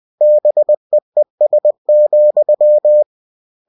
Morse code, heard as a keyed tone, about 20 wpm, 600 Hz. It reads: BEES,
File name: cw-books / winnie-the-pooh / winnie-20wpm-page-0000